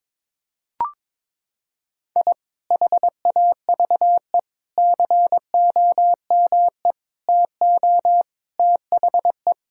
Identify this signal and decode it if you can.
Morse code, heard as a keyed tone, about 22 wpm, 700 Hz. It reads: I HAVE COME TO THE
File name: Sentences with 7-Word Phrases - 22wpm